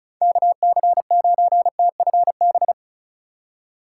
Morse code, heard as a keyed tone, about 33 wpm, 700 Hz. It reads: KC9TFB
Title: Speed Racing - USA Call Signs 35wpm